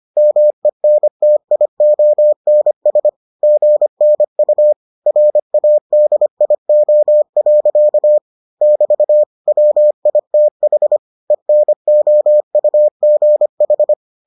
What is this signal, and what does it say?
Morse code, heard as a keyed tone, about 25 wpm, 600 Hz. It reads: MENTIONS GNU RADIO. <BT> WITH ENOUGH